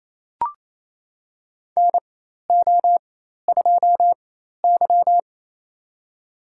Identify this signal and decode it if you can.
Morse code, heard as a keyed tone, about 28 wpm, spaced as for 13 wpm, 700 Hz. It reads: NO2Y
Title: USA Call Signs - 4 Characters - 4x Character Spacing - 28wpm